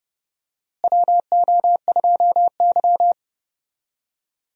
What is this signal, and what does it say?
Morse code, heard as a keyed tone, about 30 wpm, 700 Hz. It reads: WO2Y
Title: USA Call Signs - 4 Characters - Repeated 3x - 30wpm